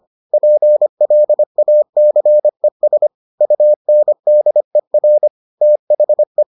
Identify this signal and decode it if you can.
Morse code, heard as a keyed tone, about 25 wpm, 600 Hz. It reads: PLACES UNDER THE